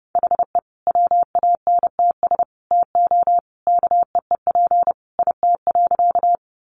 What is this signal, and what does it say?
Morse code, heard as a keyed tone, about 30 wpm, 700 Hz. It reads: HE WANTS TO KEEP IT.
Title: Additional Easy Phrases 30wpm